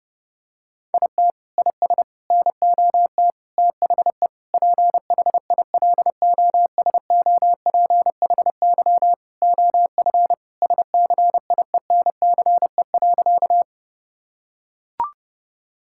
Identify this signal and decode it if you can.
Morse code, heard as a keyed tone, about 32 wpm, 700 Hz. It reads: IT IS NOT THE PHILOSOPHY OF SCIENCE.
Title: Sentences from Top 1000 Words - Review Part 6 - 30wpm